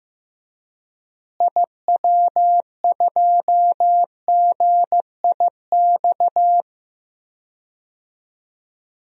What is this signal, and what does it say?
Morse code, heard as a keyed tone, about 15 wpm, 700 Hz. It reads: IW2GIX
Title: DXCC-100 Call Signs - 6 Characters - Repeated 2x - 15wpm